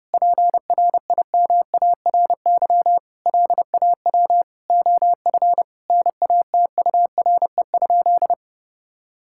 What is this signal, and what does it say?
Morse code, heard as a keyed tone, about 32 wpm, 700 Hz. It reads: PRIMARY LAW OF NATURE?
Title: Sentences from Top 1000 Words - Review Part 5 - 30wpm